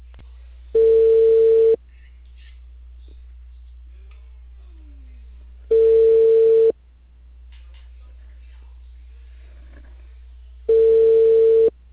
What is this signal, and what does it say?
Ringback tone (China, 450 Hz), with speech.